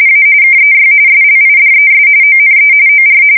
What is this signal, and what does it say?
RTTY, about 45.45 baud. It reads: A KEY SLIDES ONE OR
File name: fsk170 / morkrum